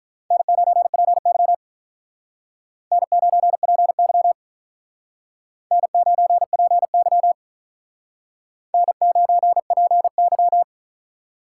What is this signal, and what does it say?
Morse code, heard as a keyed tone, about 44 wpm, 700 Hz. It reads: N9PY N9PY N9PY N9PY